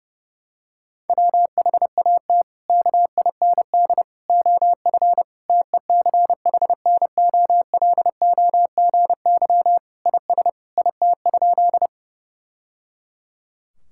Morse code, heard as a keyed tone, about 32 wpm, 700 Hz. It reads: WHAT KIND OF TECHNOLOGY IS IT?
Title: Sentences from Top 1000 Words - Review Part 5 - 30wpm